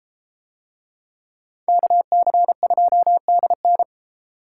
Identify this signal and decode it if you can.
Morse code, heard as a keyed tone, about 32 wpm, 700 Hz. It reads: KC2DN